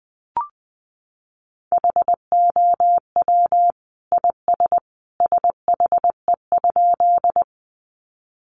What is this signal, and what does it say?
Morse code, heard as a keyed tone, about 20 wpm, 700 Hz. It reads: HOW IS SHE?
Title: Sentences from Top 100 Words 20wpm